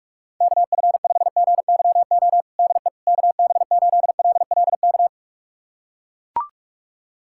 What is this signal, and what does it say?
Morse code, heard as a keyed tone, about 41 wpm, 700 Hz. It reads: KF5GYO DE KB8LLK
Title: USA Callsign DE Callsign 45wpm